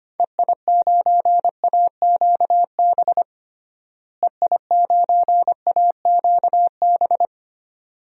Morse code, heard as a keyed tone, about 25 wpm, 700 Hz. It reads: EI9AQB EI9AQB
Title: DXCC-100 Call Signs - 6 Characters - Repeated 2x - 25wpm